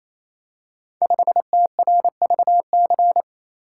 Morse code, heard as a keyed tone, about 27 wpm, 700 Hz. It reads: 5TRVC
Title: Code Groups - 5 Letter:Numbers - 28wpm